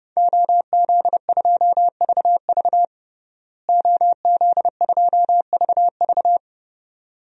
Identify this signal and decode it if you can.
Morse code, heard as a keyed tone, about 30 wpm, 700 Hz. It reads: OZ2VV OZ2VV
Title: DXCC-100 Call Signs - 5 Characters - Repeated 2x - 30wpm